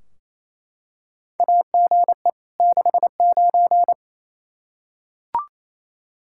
Morse code, heard as a keyed tone, about 27 wpm, 700 Hz. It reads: AGE 69
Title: Practice QSO - Exchange 3 - More Info - 28wpm